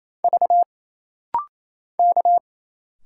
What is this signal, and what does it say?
Morse code, heard as a keyed tone, about 28 wpm, 700 Hz. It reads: V K
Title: Single Letters - Rapid-Fire 28wpm